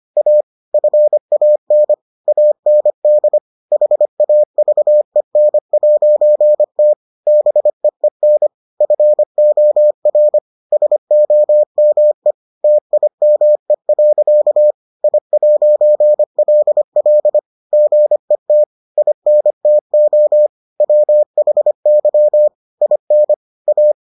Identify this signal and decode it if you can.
Morse code, heard as a keyed tone, about 25 wpm, 600 Hz. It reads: A FAN AND HAVEN'T BEEN FOR SOME TIME. I'LL GET INTO WHY IN A